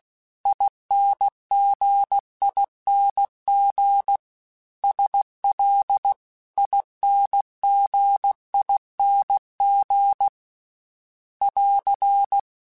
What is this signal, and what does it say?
Morse code, heard as a keyed tone, about 16 wpm, 800 Hz. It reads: INGING SL INGING <AR>
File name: Special INGs-mp3s / Special INGs-mp3s / Dble Ings sp @15